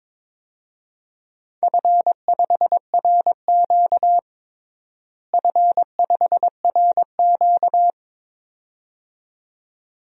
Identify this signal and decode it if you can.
Morse code, heard as a keyed tone, about 22 wpm, 700 Hz. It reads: F5RQ F5RQ
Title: DXCC-100 Call Signs - 4 Characters - Repeated 2x - 22wpm